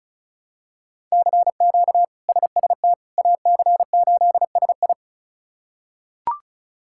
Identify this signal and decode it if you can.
Morse code, heard as a keyed tone, about 36 wpm, 700 Hz. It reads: CQ SST AC8SI
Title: K1USN SST QSO Practice - 35wpm